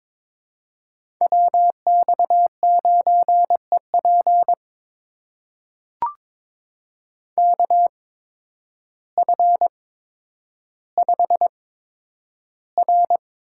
Morse code, heard as a keyed tone, about 22 wpm, spaced as for 10 wpm, 700 Hz. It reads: WX9EP KF5R
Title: USA Call Signs - 8x Character Spacing - 22wpm